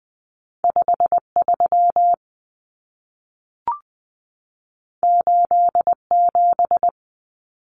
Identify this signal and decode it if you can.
Morse code, heard as a keyed tone, about 20 wpm, 700 Hz. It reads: 53 87
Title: Common Cut Numbers - 2-Digit 20wpm